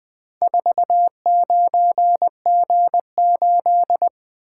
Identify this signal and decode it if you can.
Morse code, heard as a keyed tone, about 21 wpm, 700 Hz. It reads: 49G8
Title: Four Letter-Number Combinations - 20wpm